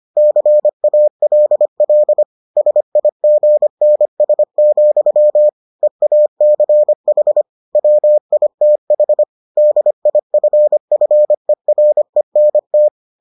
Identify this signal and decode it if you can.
Morse code, heard as a keyed tone, about 25 wpm, 600 Hz. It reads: CALL SIGNS, EACH WITH DIFFERENT